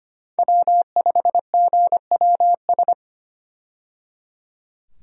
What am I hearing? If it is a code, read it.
Morse code, heard as a keyed tone, about 25 wpm, 700 Hz. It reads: W5GWS
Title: USA Call Signs - 5 Characters - 25wpm